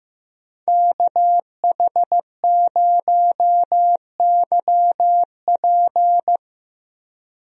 Morse code, heard as a keyed tone, about 15 wpm, 700 Hz. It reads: KH0YP
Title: USA Call Signs - 5 Characters - 15wpm